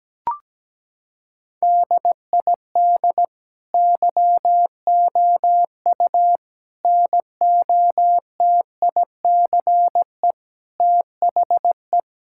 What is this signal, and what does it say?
Morse code, heard as a keyed tone, about 17 wpm, 700 Hz. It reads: DID YOU NOTICE THE
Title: Sentences from Top 1000 Words - Review Part 5 - 17wpm